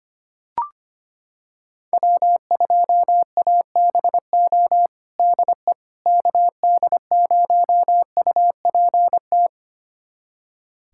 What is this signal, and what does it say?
Morse code, heard as a keyed tone, about 25 wpm, 700 Hz. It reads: W2ABO DE KD0UPT
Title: USA Callsign DE Callsign 25wpm